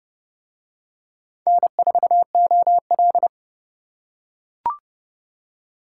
Morse code, heard as a keyed tone, about 31 wpm, 700 Hz. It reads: N4OL